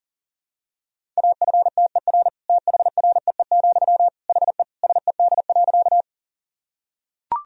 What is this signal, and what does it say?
Morse code, heard as a keyed tone, about 40 wpm, 700 Hz. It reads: AFTER THREE, HE SED.
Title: Sentences from Top 200 Words and Abbreviations 40wpm